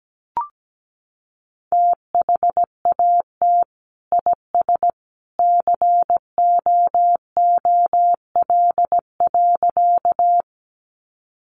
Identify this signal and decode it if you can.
Morse code, heard as a keyed tone, about 17 wpm, 700 Hz. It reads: THAT IS COOL.
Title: Sentences from Top 1200 Words 17wpm